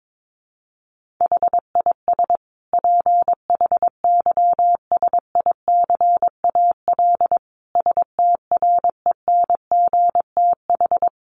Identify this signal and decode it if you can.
Morse code, heard as a keyed tone, about 22 wpm, 700 Hz. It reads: HIS PHYSICAL STRENGTH
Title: Sentences from Top 1400 Words 22wpm